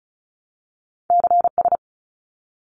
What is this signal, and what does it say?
Morse code, heard as a keyed tone, about 34 wpm, 700 Hz. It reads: CS
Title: Abbreviations 35wpm